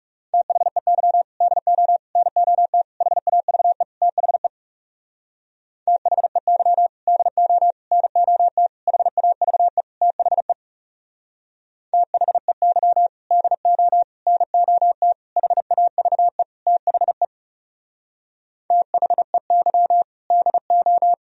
Morse code, heard as a keyed tone, about 36 wpm, 700 Hz. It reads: THEY DO NOT HAVE THE THEY DO NOT HAVE THE THEY DO NOT HAVE THE THEY DO
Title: Speed-Racing - Sets of 5 Words 30wpm